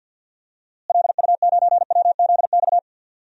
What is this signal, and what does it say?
Morse code, heard as a keyed tone, about 52 wpm, 700 Hz. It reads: LU9WZX
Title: DXCC-100 Call Signs - 6 Characters - Repeated 3x - 50wpm